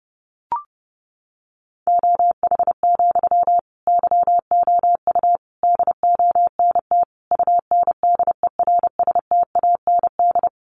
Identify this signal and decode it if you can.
Morse code, heard as a keyed tone, about 30 wpm, 700 Hz. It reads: OH, YOU DONT UNDERSTAND
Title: Sentences from Top 1000 Words 30wpm